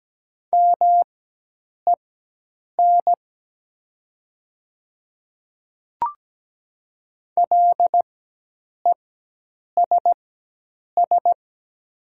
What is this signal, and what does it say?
Morse code, heard as a keyed tone, about 17 wpm, spaced as for 8 wpm, 700 Hz. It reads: MEN LESS
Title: 4x Character Spacing - Top 500 Words - 17wpm